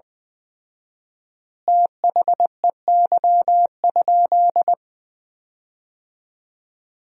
Morse code, heard as a keyed tone, about 20 wpm, 700 Hz. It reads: THEY?
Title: 4x Word-Spacing - Sentences from Top 200 Words - 20wpm